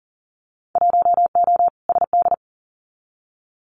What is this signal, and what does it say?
Morse code, heard as a keyed tone, about 42 wpm, 700 Hz. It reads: 1O SD
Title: Winter Field Day Exchanges Encore 40wpm